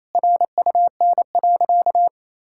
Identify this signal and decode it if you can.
Morse code, heard as a keyed tone, about 28 wpm, 700 Hz. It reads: RUN.
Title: Sentences from Top 1000 Words - Review Part 5 - 28wpm